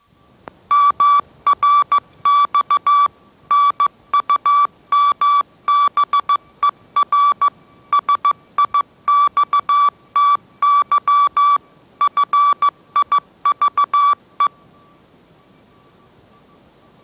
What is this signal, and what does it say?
Morse code, heard as a keyed tone, about 15 wpm, 1150 Hz. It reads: MRX NUMBER SIXTY FIVE
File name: mrx / mrx065 16wpm